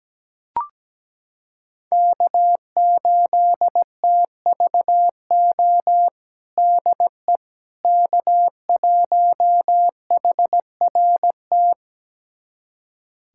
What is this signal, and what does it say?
Morse code, heard as a keyed tone, about 17 wpm, 700 Hz. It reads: K8TVO DE K1HRT
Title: USA Callsign DE Callsign - 5 Characters - 17wpm